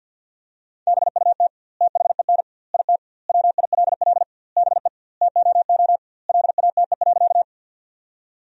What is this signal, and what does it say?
Morse code, heard as a keyed tone, about 52 wpm, 700 Hz. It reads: BUT THEN IT WILL BE TOO LATE.
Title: 2x Word-Spacing - Sentences from Top 300 Words - 50wpm